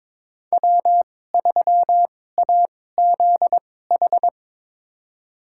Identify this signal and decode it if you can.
Morse code, heard as a keyed tone, about 22 wpm, spaced as for 16 wpm, 700 Hz. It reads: W3AZH
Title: USA Call Signs - 2x Character Spacing - 22wpm